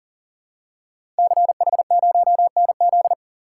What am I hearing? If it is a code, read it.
Morse code, heard as a keyed tone, about 41 wpm, 700 Hz. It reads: CH0NZ
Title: Code Groups - 5 Letter:Numbers - 40wpm